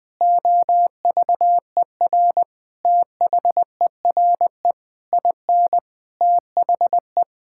Morse code, heard as a keyed tone, about 21 wpm, 700 Hz. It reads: OVER THERE IN THE
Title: Sentences from Top 1600 Words 20wpm